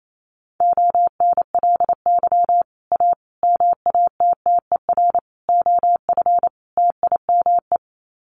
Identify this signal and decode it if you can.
Morse code, heard as a keyed tone, about 28 wpm, 700 Hz. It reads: ONLY A MATTER OF TIME
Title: Sets of 5 Words Encore 28wpm